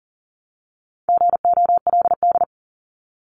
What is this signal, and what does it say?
Morse code, heard as a keyed tone, about 37 wpm, 700 Hz. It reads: GOLD
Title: Multi-letter Phonemes Repeated 3x 40wpm